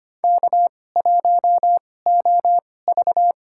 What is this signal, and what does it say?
Morse code, heard as a keyed tone, about 25 wpm, spaced as for 18 wpm, 700 Hz. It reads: K1OV
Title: USA Call Signs - 4 Characters - 2x Character Spacing - 25wpm